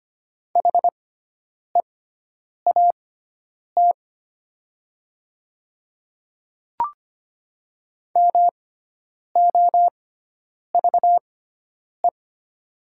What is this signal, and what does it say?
Morse code, heard as a keyed tone, about 25 wpm, spaced as for 9 wpm, 700 Hz. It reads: HEAT MOVE
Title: 6x Character Spacing - Top 500 Words - 25wpm